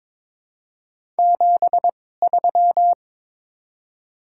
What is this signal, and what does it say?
Morse code, heard as a keyed tone, about 22 wpm, spaced as for 16 wpm, 690 Hz. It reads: 73
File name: Top 20 QSO Elements - 2x Character Spacing - 22wpm